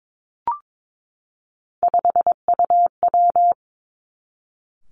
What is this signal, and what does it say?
Morse code, heard as a keyed tone, about 22 wpm, 700 Hz. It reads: 5UW
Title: Three Challenging Letter-Number Combinations 22wpm